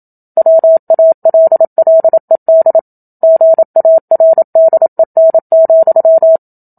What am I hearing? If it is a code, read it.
Morse code, heard as a keyed tone, about 27 wpm, 650 Hz. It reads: WALLED GARDEN,
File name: Book Two / 01-Chapter 1 - 27 WPM